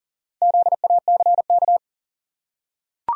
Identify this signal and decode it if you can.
Morse code, heard as a keyed tone, about 40 wpm, 700 Hz. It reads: ZACK
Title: USA Common Names - Max 5 Letters - 40wpm